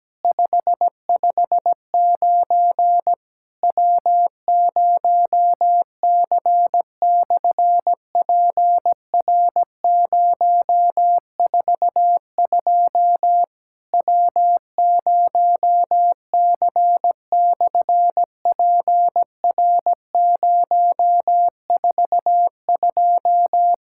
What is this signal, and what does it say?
Morse code, heard as a keyed tone, about 17 wpm, 700 Hz. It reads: 559 W0C/PR042 W0C/PR042
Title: SOTA QSO Practice 17wpm